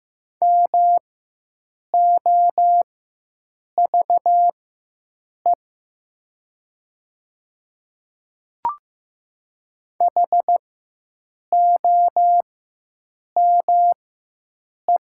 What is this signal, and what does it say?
Morse code, heard as a keyed tone, about 15 wpm, spaced as for 7 wpm, 700 Hz. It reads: MOVE HOME
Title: 4x Character Spacing - Top 200 Words - 15wpm